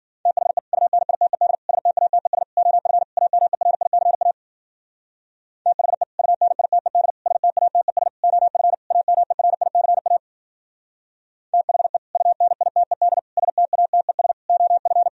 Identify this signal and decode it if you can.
Morse code, heard as a keyed tone, about 52 wpm, 700 Hz. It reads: THE UNITED STATES OF AMERICA THE UNITED STATES OF AMERICA THE UNITED STATES OF